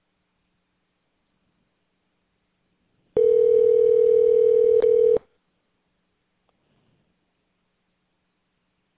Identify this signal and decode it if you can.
Ringback tone (North America), 440 + 480 Hz.